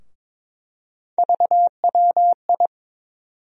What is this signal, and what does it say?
Morse code, heard as a keyed tone, about 21 wpm, 700 Hz. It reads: VWI
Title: USA Call Sign Postfixes - 3 Letters 22wpm